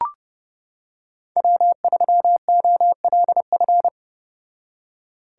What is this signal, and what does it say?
Morse code, heard as a keyed tone, about 31 wpm, 700 Hz. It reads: W3OLF